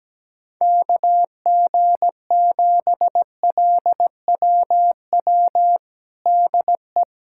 Morse code, heard as a keyed tone, about 17 wpm, 700 Hz. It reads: KG7LWW DE
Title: USA Callsign DE Callsign 17wpm